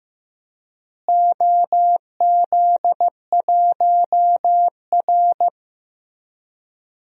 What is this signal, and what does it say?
Morse code, heard as a keyed tone, about 15 wpm, 700 Hz. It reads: OZ1R
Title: DXCC-100 Call Signs - 4 Characters - Repeated 3x - 15wpm